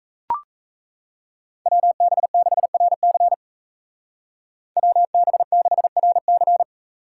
Morse code, heard as a keyed tone, about 35 wpm, 700 Hz. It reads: WB6RC WB6RC
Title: Speed Racing - USA Call Signs 28wpm